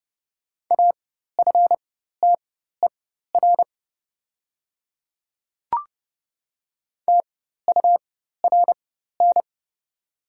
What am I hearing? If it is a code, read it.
Morse code, heard as a keyed tone, about 30 wpm, spaced as for 14 wpm, 700 Hz. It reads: AFTER TURN
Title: 4x Character Spacing - Top 300 Words - 30wpm